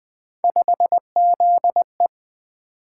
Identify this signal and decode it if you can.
Morse code, heard as a keyed tone, about 20 wpm, 700 Hz. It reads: 5ZE
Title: Three Letter-Number Combinations - 20wpm